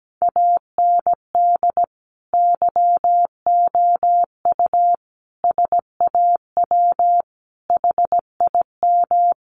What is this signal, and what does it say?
Morse code, heard as a keyed tone, about 17 wpm, 700 Hz. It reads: AND YOU SAW HIM
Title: Sentences from Top 1300 Words 17wpm